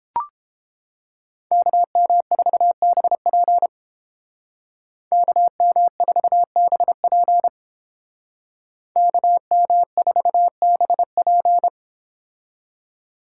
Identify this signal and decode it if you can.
Morse code, heard as a keyed tone, about 30 wpm, 700 Hz. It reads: KM4BP KM4BP KM4BP